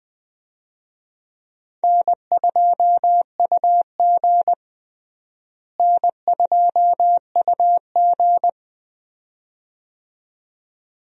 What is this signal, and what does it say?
Morse code, heard as a keyed tone, about 20 wpm, 700 Hz. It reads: N2UG N2UG
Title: USA Call Signs - 4 Characters - Repeated 2x - 20wpm